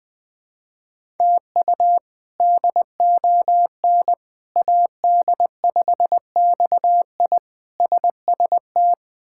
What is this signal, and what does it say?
Morse code, heard as a keyed tone, about 20 wpm, 700 Hz. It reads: TU DON AD5XI SST
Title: K1USN SST QSO Practice - 20wpm